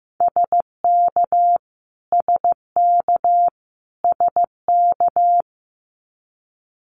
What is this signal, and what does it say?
Morse code, heard as a keyed tone, about 15 wpm, 700 Hz. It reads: SK SK SK
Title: Canadian Provinces and Territories - Repeated 3x - 15wpm